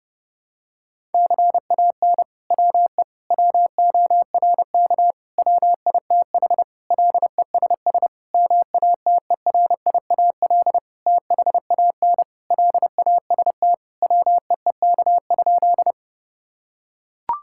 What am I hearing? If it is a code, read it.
Morse code, heard as a keyed tone, about 30 wpm, 700 Hz. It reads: CAN WE WORK WITH LESS MATERIAL THAN LAST WEEK?
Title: Sentences from Top 1000 Words - Review Part 5 - 30wpm